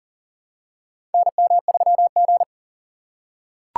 Morse code, heard as a keyed tone, about 40 wpm, 700 Hz. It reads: NM3G
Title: USA Call Signs - 4 Characters - 6x Character Spacing - 40wpm